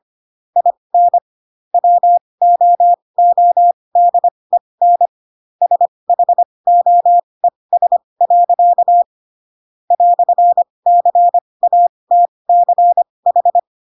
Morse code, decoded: IN WOODEN SHOES. "CATCH

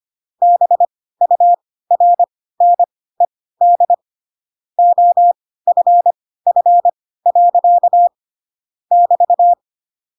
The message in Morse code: BURNED OFF. <BT>

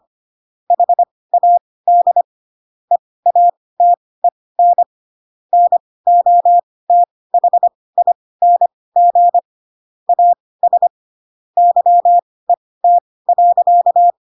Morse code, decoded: HAD EATEN NOTHING AS YET.

